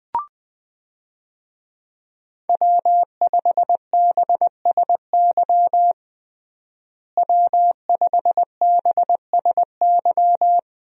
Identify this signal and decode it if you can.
Morse code, heard as a keyed tone, about 20 wpm, 700 Hz. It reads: W5BSY W5BSY
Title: USA Call Signs - 5 Characters - Repeated 2x - 20wpm